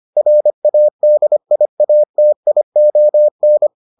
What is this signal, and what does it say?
Morse code, heard as a keyed tone, about 25 wpm, 600 Hz. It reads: RADIATION